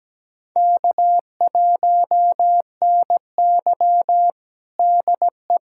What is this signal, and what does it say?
Morse code, heard as a keyed tone, about 17 wpm, 700 Hz. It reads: K1NY DE
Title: USA Callsign DE Callsign - 4 Characters - 17wpm